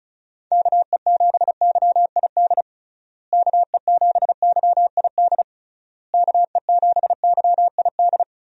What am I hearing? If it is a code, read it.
Morse code, heard as a keyed tone, about 36 wpm, 700 Hz. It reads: KE7YID KE7YID KE7YID
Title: USA Call Signs - 6 Characters - Repeated 3x - 35wpm